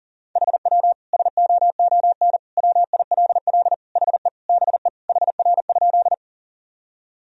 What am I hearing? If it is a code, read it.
Morse code, heard as a keyed tone, about 37 wpm, 700 Hz. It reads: HW SOON WILL HE BE HR?
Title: Sentences from Top 400 Words and Abbreviations 40wpm